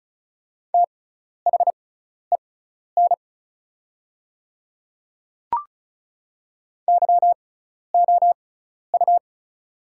Morse code, heard as a keyed tone, about 36 wpm, spaced as for 12 wpm, 700 Hz. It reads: THEN YOU
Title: 6x Character Spacing - Top 100 Words - 35wpm